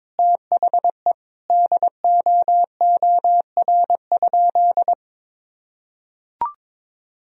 Morse code, heard as a keyed tone, about 22 wpm, 700 Hz. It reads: THE DOOR?
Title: Sentences from Top 1000 Words - Review Part 4 - 22wpm